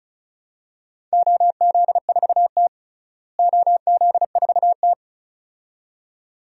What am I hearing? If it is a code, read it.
Morse code, heard as a keyed tone, about 34 wpm, 700 Hz. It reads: OZ4T OZ4T
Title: DXCC-100 Call Signs - 4 Characters - Repeated 2x - 35wpm